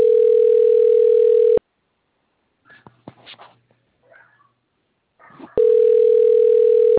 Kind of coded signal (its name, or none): telephone call progress tone